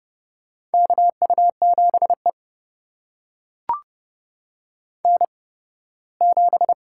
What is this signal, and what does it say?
Morse code, heard as a keyed tone, about 30 wpm, spaced as for 18 wpm, 700 Hz. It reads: KU7E N7
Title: USA Call Signs - 4 Characters - 8x Character Spacing - 30wpm